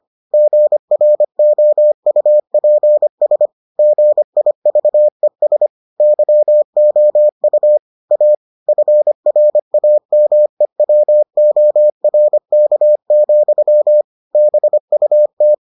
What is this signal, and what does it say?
Morse code, heard as a keyed tone, about 25 wpm, 600 Hz. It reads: GROUPS GIVES YOU A FRAMEWORK, BUT